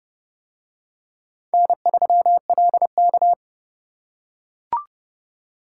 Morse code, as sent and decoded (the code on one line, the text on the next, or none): -. ...-- .-.. -.-
N3LK